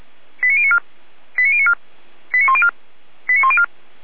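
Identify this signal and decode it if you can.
Five-tone selective call (ZVEI), heard as quick runs of five tones, tones of about 70 ms. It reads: 89094 89094 89104 89104
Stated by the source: ZVEI five-tone call received by SDR, fire brigade Orlamünde (feuerwehr-orlamuende) recording.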